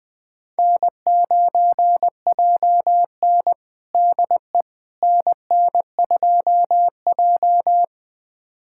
Morse code, heard as a keyed tone, about 20 wpm, 700 Hz. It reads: N9JN DE NN2J
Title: USA Callsign DE Callsign - 4 Characters - 20wpm